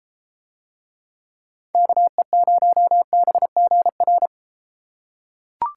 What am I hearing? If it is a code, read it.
Morse code, heard as a keyed tone, about 31 wpm, 700 Hz. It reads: KE0BGR